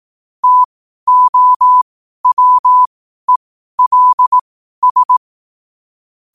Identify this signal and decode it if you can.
Morse code, heard as a keyed tone, about 18 wpm, spaced as for 12 wpm, 1000 Hz. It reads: TOWELS
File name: qotdfiles / 12wpm